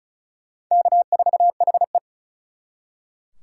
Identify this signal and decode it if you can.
Morse code, heard as a keyed tone, about 44 wpm, 700 Hz. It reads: K4HE